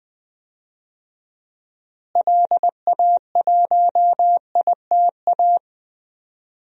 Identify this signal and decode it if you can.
Morse code, heard as a keyed tone, about 21 wpm, 700 Hz. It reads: LA1ITA
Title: DXCC-100 Call Signs - 6 Characters - Repeated 3x - 20wpm